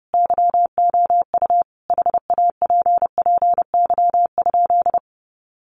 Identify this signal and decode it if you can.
Morse code, heard as a keyed tone, about 30 wpm, 700 Hz. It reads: YOU HAPPY?
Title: Sentences from Top 1000 Words 30wpm